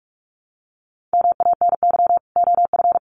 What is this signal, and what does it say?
Morse code, heard as a keyed tone, about 44 wpm, 700 Hz. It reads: MANY OF
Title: Sets of 2 Words 45wpm